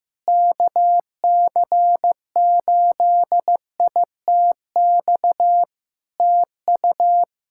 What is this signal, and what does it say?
Morse code, heard as a keyed tone, about 16 wpm, 700 Hz. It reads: KC8ITX TU
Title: Practice QSO - Exchange 4 - Ending - 15wpm